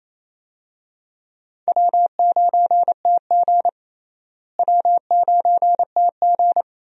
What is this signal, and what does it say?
Morse code, heard as a keyed tone, about 28 wpm, 700 Hz. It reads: W9TG W9TG
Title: USA Call Signs - 4 Characters - Repeated 2x - 28wpm